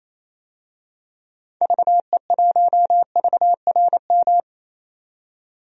Morse code, heard as a keyed tone, about 28 wpm, 700 Hz. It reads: VE1VRM
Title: DXCC-100 Call Signs - 6 Characters - 28wpm